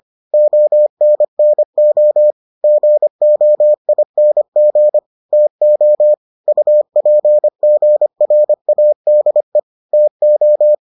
Morse code, heard as a keyed tone, about 25 wpm, 600 Hz. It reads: ONNO GOING TO UPGRADE TO